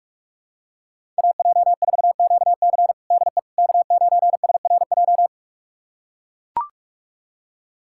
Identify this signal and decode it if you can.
Morse code, heard as a keyed tone, about 43 wpm, 700 Hz. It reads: AJ4QC DE K9SRJ
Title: USA Callsign DE Callsign - 5 Characters - 45wpm